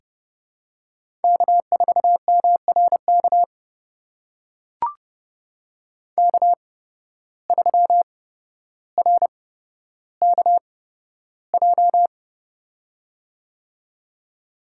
Morse code, heard as a keyed tone, about 30 wpm, spaced as for 13 wpm, 700 Hz. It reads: K4MRK K3RKJ
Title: USA Call Signs - 8x Character Spacing - 30wpm